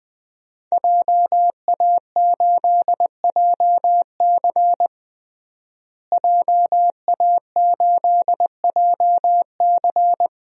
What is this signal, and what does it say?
Morse code, heard as a keyed tone, about 20 wpm, 700 Hz. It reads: JA8JC JA8JC
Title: DXCC-100 Call Signs - 5 Characters - Repeated 2x - 20wpm